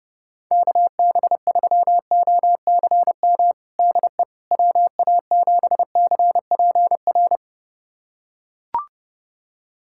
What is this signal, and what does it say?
Morse code, heard as a keyed tone, about 30 wpm, 700 Hz. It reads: KB3OCM DE WA7CPR
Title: USA Callsign DE Callsign 30wpm